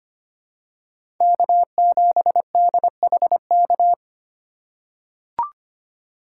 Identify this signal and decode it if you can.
Morse code, heard as a keyed tone, about 25 wpm, 700 Hz. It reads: K7DHK
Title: USA Call Signs - 5 Characters - 3x Character Spacing - 25wpm